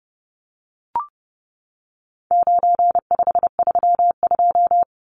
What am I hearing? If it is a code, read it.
Morse code, heard as a keyed tone, about 30 wpm, 700 Hz. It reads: 9532
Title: Numbers - 4-Digit 30wpm